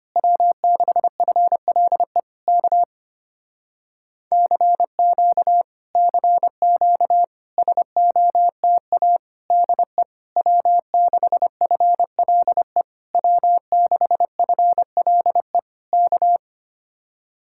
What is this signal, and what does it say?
Morse code, heard as a keyed tone, about 25 wpm, 700 Hz. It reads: W6FLE K CQ CQ SOTA DE W6FLE W6FLE K
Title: Speed-Racing Common QSO Phrases - 25wpm